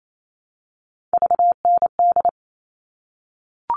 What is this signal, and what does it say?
Morse code, heard as a keyed tone, about 28 wpm, 700 Hz. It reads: VND
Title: All Cut Numbers - 3-Digit 28wpm